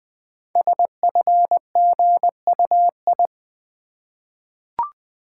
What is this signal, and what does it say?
Morse code, heard as a keyed tone, about 20 wpm, 700 Hz. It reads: SFGUI
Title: Code Groups - 5 Letter:Numbers - 20wpm